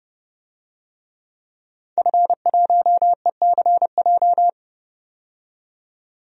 Morse code, heard as a keyed tone, about 30 wpm, 700 Hz. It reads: F1ECJ
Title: DXCC-100 Call Signs - 5 Characters - 30wpm